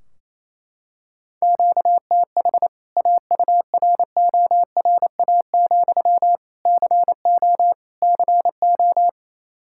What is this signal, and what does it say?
Morse code, heard as a keyed tone, about 28 wpm, 700 Hz. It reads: QTH AURORA, CO CO
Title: K1USN SST QSO Practice 28wpm